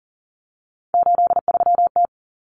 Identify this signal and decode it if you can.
Morse code, heard as a keyed tone, about 40 wpm, 700 Hz. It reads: 83T
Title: Common Cut Numbers - 3-Digit 40wpm